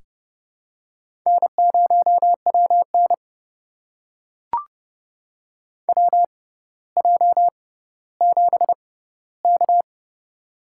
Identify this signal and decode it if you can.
Morse code, heard as a keyed tone, about 31 wpm, spaced as for 16 wpm, 700 Hz. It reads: N0WN WJ7K
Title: USA Call Signs - 4 Characters - 6x Character Spacing - 30wpm